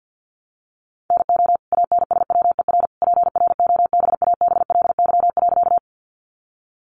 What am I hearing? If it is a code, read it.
Morse code, heard as a keyed tone, about 50 wpm, 700 Hz. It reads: NO ANSWER PROBABLY.
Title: Sentences from Top 700 Words 50wpm